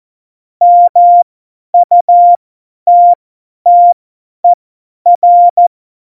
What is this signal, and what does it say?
Morse code, heard as a keyed tone, about 14 wpm, spaced as for 10 wpm, 700 Hz. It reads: MUTTER
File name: Morsekurs / Witze / Witz 18. September 2018, 50 F70